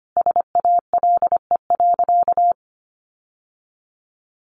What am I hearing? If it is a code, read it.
Morse code, heard as a keyed tone, about 25 wpm, 700 Hz. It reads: SALE.
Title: Sentences from Top 1400 Words 25wpm